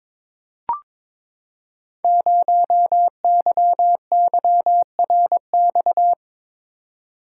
Morse code, heard as a keyed tone, about 23 wpm, 700 Hz. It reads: 0YYRX
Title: Code Groups - 5 Letter:Numbers - 22wpm